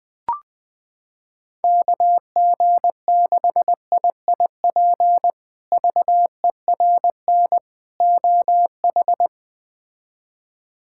Morse code, heard as a keyed tone, about 21 wpm, 700 Hz. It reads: KG6IIP VERN OH
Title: K1USN SST QSO Practice - 20wpm